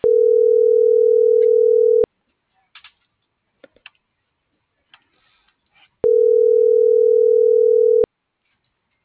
Ringback tone (North America), 440 + 480 Hz.
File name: CENTRAL MULTIMODAL / out josimar.cruz - .59358